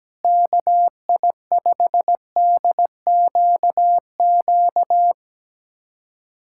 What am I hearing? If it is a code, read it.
Morse code, heard as a keyed tone, about 17 wpm, 700 Hz. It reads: KI5DQQ
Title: USA Callsign DE Callsign - 6 Characters - 17wpm